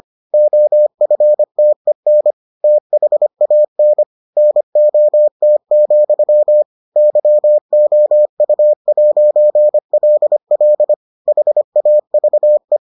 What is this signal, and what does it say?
Morse code, heard as a keyed tone, about 25 wpm, 600 Hz. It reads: OFTEN THAN NOT, YOU'LL HAVE